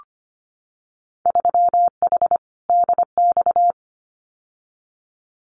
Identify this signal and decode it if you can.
Morse code, heard as a keyed tone, about 25 wpm, 690 Hz. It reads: 3H DX
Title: Winter Field Day Exchanges Encore 25wpm